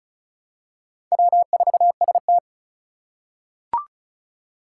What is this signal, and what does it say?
Morse code, heard as a keyed tone, about 36 wpm, 700 Hz. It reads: W4ST
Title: Four Challenging Letter-Number Combinations 35wpm